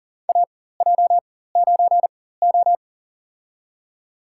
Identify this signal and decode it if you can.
Morse code, heard as a keyed tone, about 40 wpm, spaced as for 19 wpm, 700 Hz. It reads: AJ9O
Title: USA Call Signs - 4 Characters - 4x Character Spacing - 40wpm